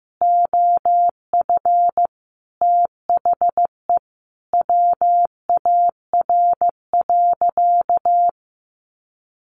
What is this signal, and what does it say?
Morse code, heard as a keyed tone, about 15 wpm, 700 Hz. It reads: OF THE WAR.
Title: Sentences from Top 1400 Words 15wpm